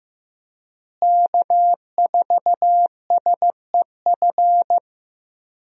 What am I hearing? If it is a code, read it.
Morse code, heard as a keyed tone, about 15 wpm, 700 Hz. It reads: K4SEF
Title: USA Call Signs - 5 Characters - Repeated 2x - 15wpm